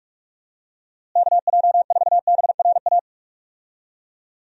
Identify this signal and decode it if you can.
Morse code, heard as a keyed tone, about 43 wpm, 700 Hz. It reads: KJ4BRA
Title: USA Call Signs - 6 Characters - Repeated 3x - 45wpm